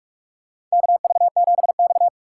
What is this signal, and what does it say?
Morse code, heard as a keyed tone, about 44 wpm, 700 Hz. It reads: KV7X
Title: USA Call Signs - 4 Characters - 8x Character Spacing - 45wpm